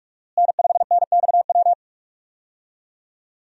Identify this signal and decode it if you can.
Morse code, heard as a keyed tone, about 42 wpm, 700 Hz. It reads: N5NXW
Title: USA Call Signs - 5 Characters - Repeated 2x - 45wpm